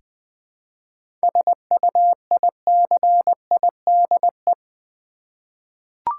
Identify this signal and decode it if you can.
Morse code, heard as a keyed tone, about 20 wpm, 700 Hz. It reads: SUICIDE
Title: 7 Letter Words - First 3 Letters Given - 20wpm